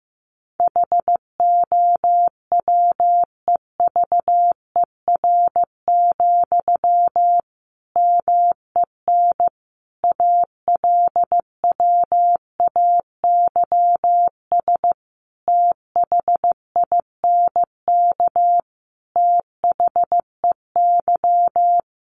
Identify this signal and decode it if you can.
Morse code, heard as a keyed tone, about 15 wpm, 700 Hz. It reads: HOWEVER, MEN ALWAYS THINK THEY
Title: Sentences from Top 600 Words 15wpm